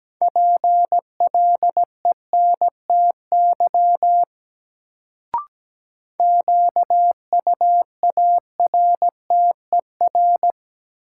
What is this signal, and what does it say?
Morse code, heard as a keyed tone, about 17 wpm, 700 Hz. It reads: PLENTY QUARTER
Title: Sentences from Top 1300 Words - New Word Drill - 17wpm